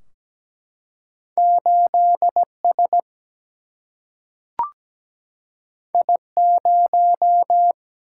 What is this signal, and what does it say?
Morse code, heard as a keyed tone, about 17 wpm, 700 Hz. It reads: 8 S I 0
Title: Single-Letter Single-Number 17wpm